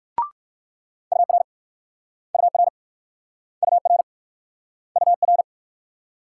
Morse code, heard as a keyed tone, about 57 wpm, 690 Hz. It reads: UR UR UR UR